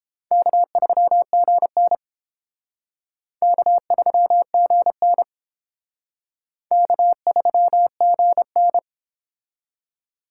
Morse code, heard as a keyed tone, about 29 wpm, 700 Hz. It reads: K3GN K3GN K3GN